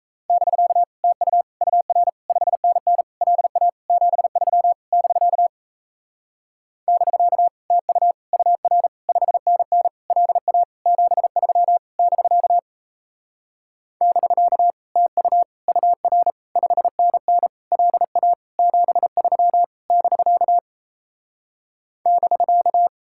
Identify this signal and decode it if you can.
Morse code, heard as a keyed tone, about 37 wpm, 700 Hz. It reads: <BK> TU UR 5NN LA 73 <BK> <BK> TU UR 5NN LA 73 <BK> <BK> TU UR 5NN LA 73 <BK> <BK>